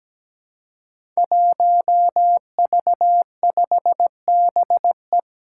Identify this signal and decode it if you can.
Morse code, heard as a keyed tone, about 17 wpm, 700 Hz. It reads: 1V5BE
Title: Five Challenging Letter-Number Combinations 17wpm